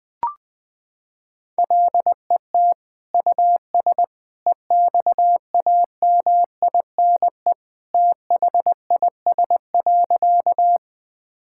Morse code, heard as a keyed tone, about 20 wpm, 700 Hz. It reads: LET US EXAMINE THIS.
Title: Sentences from Top 1600 Words 20wpm